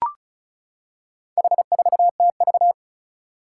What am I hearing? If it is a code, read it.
Morse code, heard as a keyed tone, about 36 wpm, 700 Hz. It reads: H4TV
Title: Four Challenging Letter-Number Combinations 35wpm